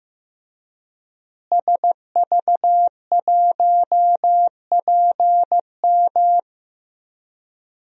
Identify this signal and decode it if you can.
Morse code, heard as a keyed tone, about 15 wpm, 700 Hz. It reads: SV1PM
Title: DXCC-100 Call Signs - 5 Characters - Repeated 3x - 15wpm